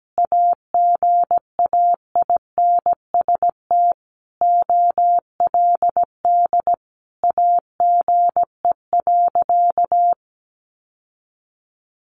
Morse code, heard as a keyed tone, about 17 wpm, 700 Hz. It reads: AGAINST OLD AGE.
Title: Sentences from Top 1100 Words 17wpm